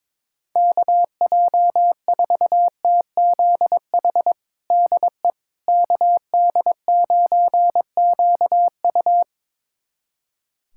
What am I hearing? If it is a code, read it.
Morse code, heard as a keyed tone, about 22 wpm, 700 Hz. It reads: KJ4TZH DE KD9QU
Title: USA Callsign DE Callsign 22wpm